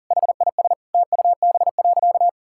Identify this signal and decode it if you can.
Morse code, heard as a keyed tone, about 40 wpm, 700 Hz. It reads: HIS TUB.
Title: Pangram Challenge v2 - 40wpm